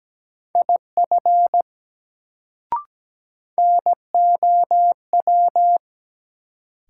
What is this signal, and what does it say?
Morse code, heard as a keyed tone, about 17 wpm, 700 Hz. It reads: IF NOW
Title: Sets of 2 Words - New Word Drill - 17wpm